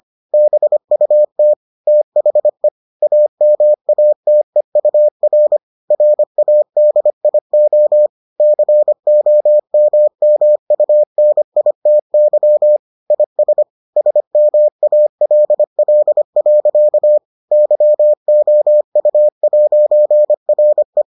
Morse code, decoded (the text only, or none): BUT THE AMATEUR RADIO COMMUNITY IS SMALL. YOU'RE